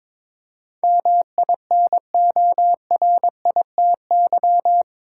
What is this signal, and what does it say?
Morse code, heard as a keyed tone, about 22 wpm, 700 Hz. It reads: MINORITY
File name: practice / 22wpm / Eight Letter Words - 22wpm